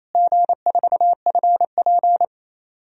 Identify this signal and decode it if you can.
Morse code, heard as a keyed tone, about 28 wpm, 700 Hz. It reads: G4FP
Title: DXCC-100 Call Signs - 4 Characters - Repeated 3x - 28wpm